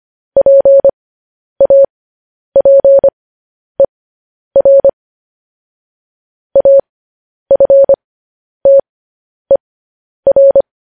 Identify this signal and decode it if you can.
Morse code, heard as a keyed tone, about 25 wpm, spaced as for 10 wpm, 550 Hz. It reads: PAPER AFTER
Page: news-headlines-in-morse-code-at-10-wpm